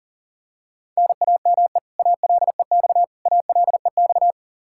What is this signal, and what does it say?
Morse code, heard as a keyed tone, about 40 wpm, 700 Hz. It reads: NAME ALEX ALEX
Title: Practice QSO - Exchange 1 - Beginning - 40wpm